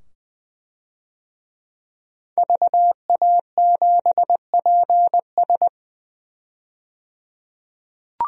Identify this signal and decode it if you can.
Morse code, heard as a keyed tone, about 20 wpm, 700 Hz. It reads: VA7PS
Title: DXCC-100 Call Signs - 5 Characters - 20wpm